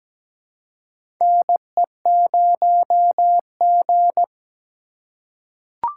Morse code, heard as a keyed tone, about 17 wpm, 700 Hz. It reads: NE0G
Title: USA Call Signs - 4 Characters - 2x Character Spacing - 17wpm